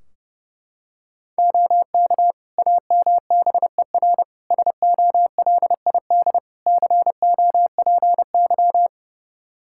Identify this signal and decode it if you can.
Morse code, heard as a keyed tone, about 30 wpm, 700 Hz. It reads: OK AMBER SOLID COPY
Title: Practice QSO - Exchange 3 - More Info - 30wpm